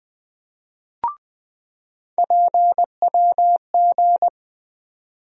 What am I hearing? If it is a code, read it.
Morse code, heard as a keyed tone, about 20 wpm, 700 Hz. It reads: PWG
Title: USA Call Sign Postfixes - 3 Letters 20wpm